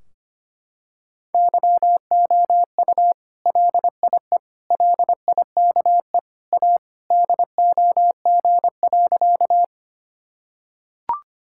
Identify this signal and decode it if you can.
Morse code, heard as a keyed tone, about 25 wpm, 700 Hz. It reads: YOU LIE LIKE A DOG.
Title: Sentences from Top 1600 Words 25wpm